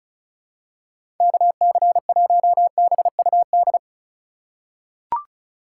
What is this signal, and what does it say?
Morse code, heard as a keyed tone, about 34 wpm, 700 Hz. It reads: KC1BUD
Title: USA Call Signs - 6 Characters - 3x Character Spacing - 35wpm